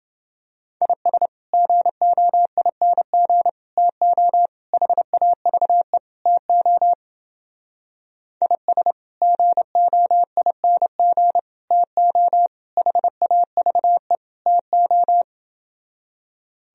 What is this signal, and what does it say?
Morse code, heard as a keyed tone, about 30 wpm, 700 Hz. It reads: IS GOING TO HAVE TO IS GOING TO HAVE TO